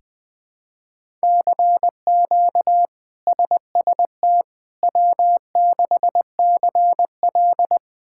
Morse code, heard as a keyed tone, about 20 wpm, 700 Hz. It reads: CQ SST W6CL
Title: K1USN SST QSO Practice - 20wpm